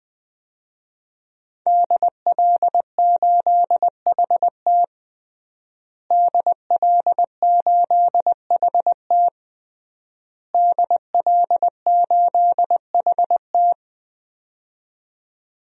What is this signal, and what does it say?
Morse code, heard as a keyed tone, about 20 wpm, 700 Hz. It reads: DL8HT DL8HT DL8HT
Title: DXCC-100 Call Signs - 5 Characters - Repeated 3x - 20wpm